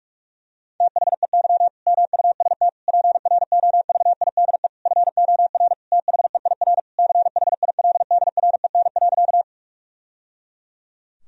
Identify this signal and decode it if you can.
Morse code, heard as a keyed tone, about 46 wpm, 700 Hz. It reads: THEY MUST PROVIDE FOR THEIR CHILDREN.
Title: Sentences from Top 1000 Words - Review Part 3 - 45wpm